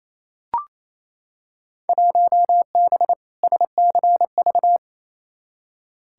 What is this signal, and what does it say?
Morse code, heard as a keyed tone, about 28 wpm, 700 Hz. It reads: 1B SCV
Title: Field Day Exchanges Encore - 28wpm